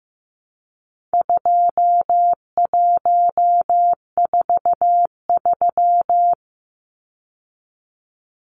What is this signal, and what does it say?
Morse code, heard as a keyed tone, about 15 wpm, 700 Hz. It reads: 2143
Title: Numbers - 4-Digit 15wpm